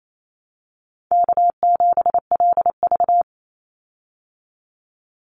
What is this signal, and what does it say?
Morse code, heard as a keyed tone, about 28 wpm, 700 Hz. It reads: K7LV
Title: Call Signs Repeated 2x - 28wpm